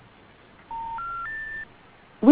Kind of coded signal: telephone call progress tone